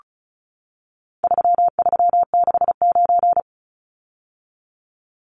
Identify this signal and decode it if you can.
Morse code, heard as a keyed tone, about 34 wpm, 700 Hz. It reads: 3369
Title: Numbers - 4-Digit 35wpm